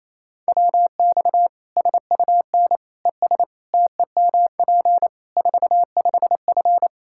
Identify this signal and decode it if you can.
Morse code, heard as a keyed tone, about 28 wpm, 700 Hz. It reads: WX SUN ES TEMP 45F
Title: Practice QSO - Exchange 2 - Info - 28wpm